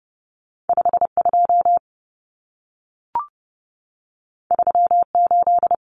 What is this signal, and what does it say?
Morse code, heard as a keyed tone, about 30 wpm, 700 Hz. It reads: 52 38
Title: Numbers - 2-Digit 30wpm